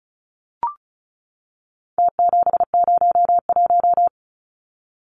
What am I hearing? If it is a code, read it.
Morse code, heard as a keyed tone, about 34 wpm, 700 Hz. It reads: T701
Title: Numbers - Contest Style - 4-Digit 35wpm